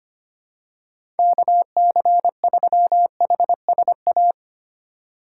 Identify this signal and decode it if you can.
Morse code, heard as a keyed tone, about 25 wpm, 700 Hz. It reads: KC3HSA
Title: USA Call Signs - 6 Characters - 3x Character Spacing - 25wpm